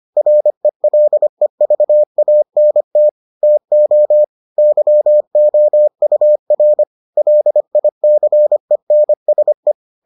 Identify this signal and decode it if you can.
Morse code, heard as a keyed tone, about 25 wpm, 600 Hz. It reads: RELEVANT TO YOUR LICENSE